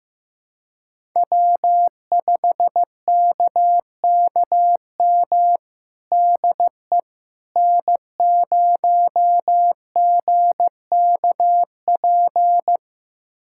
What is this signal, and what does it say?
Morse code, heard as a keyed tone, about 15 wpm, 700 Hz. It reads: W5KKM DE N0GKP
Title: USA Callsign DE Callsign - 5 Characters - 15wpm